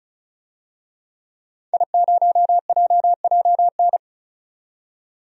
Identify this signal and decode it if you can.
Morse code, heard as a keyed tone, about 36 wpm, 700 Hz. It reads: I0JJN
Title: DXCC-100 Call Signs - 5 Characters - 35wpm